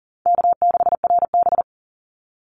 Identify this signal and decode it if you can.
Morse code, heard as a keyed tone, about 40 wpm, 700 Hz. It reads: K6RB
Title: Easy Call Signs Repeated 2x - 40wpm